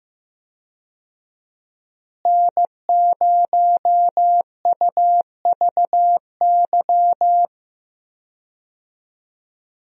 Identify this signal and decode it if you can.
Morse code, heard as a keyed tone, about 15 wpm, 700 Hz. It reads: N0UVY
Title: USA Call Signs - 5 Characters - Repeated 2x - 15wpm